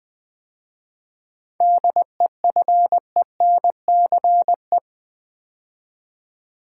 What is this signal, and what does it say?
Morse code, heard as a keyed tone, about 20 wpm, 700 Hz. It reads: DEFENCE
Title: 7 Letter Words - First 3 Letters Given - 20wpm